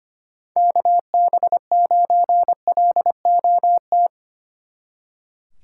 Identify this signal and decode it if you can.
Morse code, heard as a keyed tone, about 32 wpm, 700 Hz. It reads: KB9LOT